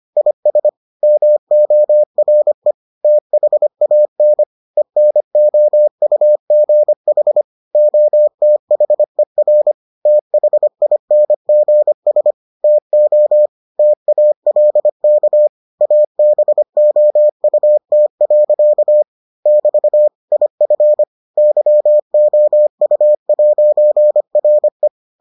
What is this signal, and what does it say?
Morse code, heard as a keyed tone, about 25 wpm, 600 Hz. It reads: IS MORE THAN ENOUGH OTHER THINGS TO TALK ABOUT. <BT> IF YOU'RE